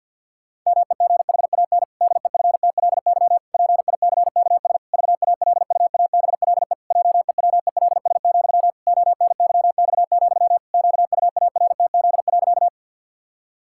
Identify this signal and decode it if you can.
Morse code, heard as a keyed tone, about 50 wpm, 700 Hz. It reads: MEGHAN DEFTLY PICKS VALUABLE JEWELS, ONYX, QUARTZ.
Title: Pangram Challenge v2 - 50wpm